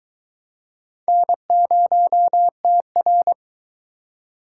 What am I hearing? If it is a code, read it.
Morse code, heard as a keyed tone, about 22 wpm, 700 Hz. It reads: N0TR